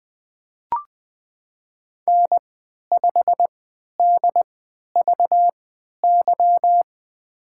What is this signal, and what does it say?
Morse code, heard as a keyed tone, about 20 wpm, spaced as for 11 wpm, 700 Hz. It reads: N5DVY
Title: USA Call Signs - 5 Characters - 3x Character Spacing - 20wpm